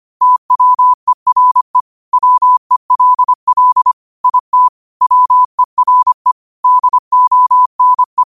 Morse code, heard as a keyed tone, about 25 wpm, 1000 Hz. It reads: TWERE WELL IT WERE DONE
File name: qotdfiles / 25wpm